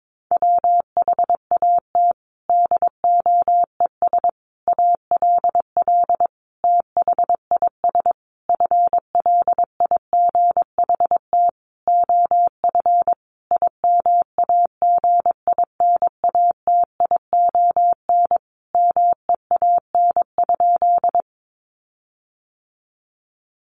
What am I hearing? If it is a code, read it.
Morse code, heard as a keyed tone, about 22 wpm, 700 Hz. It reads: WHAT DOES ALL THIS FLIGHT OF IMAGINATION MEAN?
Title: Sentences from Top 1300 Words 22wpm